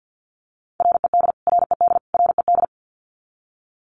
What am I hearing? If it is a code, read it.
Morse code, heard as a keyed tone, about 50 wpm, 700 Hz. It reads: RED RED RED
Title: Top 500 Words Repeated 50wpm